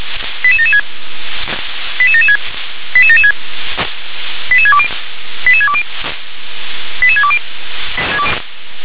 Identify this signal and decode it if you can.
Five-tone selective call (ZVEI), heard as quick runs of five tones, tones of about 70 ms. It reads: 88776 88776 88776 88620 88620 88620
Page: /357/SDR/ZVEI/